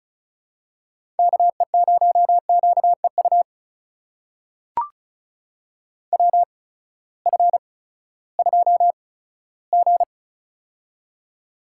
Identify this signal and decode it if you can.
Morse code, heard as a keyed tone, about 36 wpm, spaced as for 18 wpm, 700 Hz. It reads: KE0QEU WF2G
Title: USA Call Signs - 8x Character Spacing - 35wpm